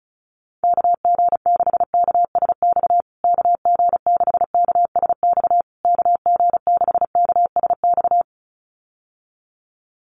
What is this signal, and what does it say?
Morse code, heard as a keyed tone, about 34 wpm, 700 Hz. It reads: KG6KSX KG6KSX KG6KSX
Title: USA Call Signs Repeated 3x 35wpm